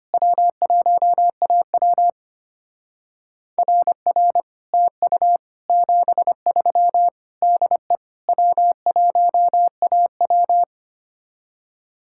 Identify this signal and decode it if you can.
Morse code, heard as a keyed tone, about 32 wpm, 700 Hz. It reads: W1AW RR TU 73 DE W1AW